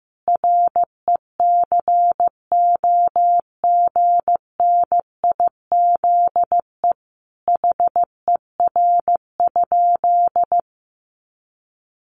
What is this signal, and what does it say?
Morse code, heard as a keyed tone, about 15 wpm, 700 Hz. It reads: RECOGNIZE HER?
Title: Sentences from Top 1400 Words 15wpm